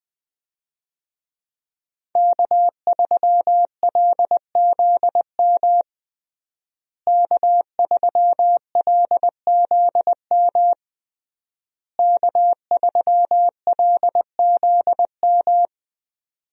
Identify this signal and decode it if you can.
Morse code, heard as a keyed tone, about 21 wpm, 700 Hz. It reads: K3LZM K3LZM K3LZM
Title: USA Call Signs - 5 Characters - Repeated 3x - 20wpm